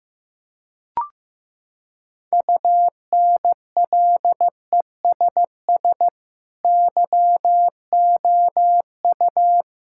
Morse code, heard as a keyed tone, about 15 wpm, 700 Hz. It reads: UNLESS YOU
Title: Sentences from Top 1000 Words - Review Part 6 - 15wpm